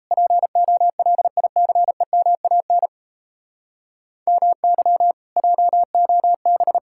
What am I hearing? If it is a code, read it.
Morse code, heard as a keyed tone, about 37 wpm, 700 Hz. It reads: POLICEMAN MY JOB